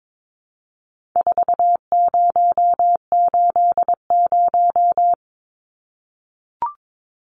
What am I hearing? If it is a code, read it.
Morse code, heard as a keyed tone, about 22 wpm, 700 Hz. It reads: 4080
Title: Numbers - 4-Digit 22wpm